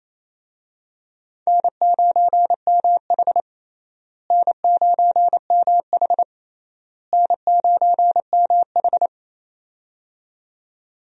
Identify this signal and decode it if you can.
Morse code, heard as a keyed tone, about 28 wpm, 700 Hz. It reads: N9MH N9MH N9MH
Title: USA Call Signs - 4 Characters - Repeated 3x - 28wpm